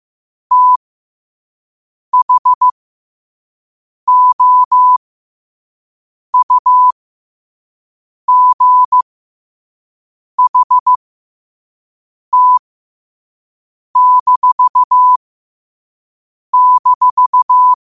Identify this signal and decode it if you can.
Morse code, heard as a keyed tone, about 15 wpm, spaced as for 5 wpm, 1000 Hz. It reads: THOUGHT--
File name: qotdfiles / 5wpm